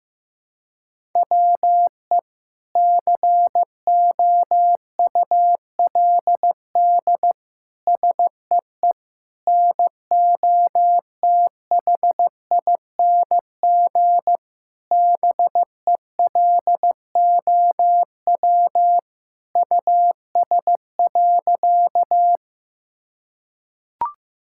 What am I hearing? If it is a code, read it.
Morse code, heard as a keyed tone, about 16 wpm, 700 Hz. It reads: WE COULD SEE NOTHING BELOW US.
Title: Sentences from Top 1000 Words - Review Part 5 - 15wpm